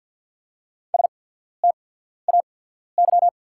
Morse code, heard as a keyed tone, about 50 wpm, spaced as for 14 wpm, 700 Hz. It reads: STAY
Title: 8x Character Spacing - Top 500 Words - 50wpm